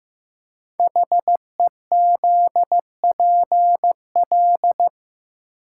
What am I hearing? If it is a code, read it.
Morse code, heard as a keyed tone, about 15 wpm, 700 Hz. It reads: HEZPL
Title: Code Groups - 5 Letters - 15wpm